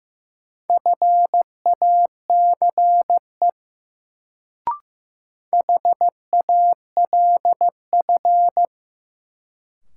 Morse code, heard as a keyed tone, about 15 wpm, 700 Hz. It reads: FACE HALF
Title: Sets of 3 Words - New Word Drill - 15wpm